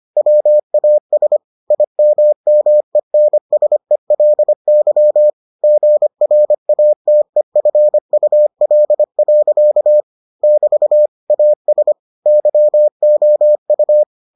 Morse code, heard as a keyed tone, about 25 wpm, 600 Hz. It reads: WAS IMMENSELY GRATEFUL. <BT> AS YOU